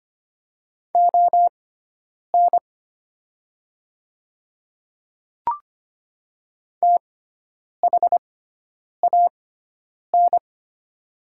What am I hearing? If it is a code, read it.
Morse code, heard as a keyed tone, about 25 wpm, spaced as for 9 wpm, 700 Hz. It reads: ON THAN
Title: 6x Character Spacing - Top 100 Words - 25wpm